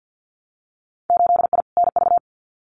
Morse code, heard as a keyed tone, about 50 wpm, 700 Hz. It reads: 8I NV
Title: Winter Field Day Exchanges Encore 50wpm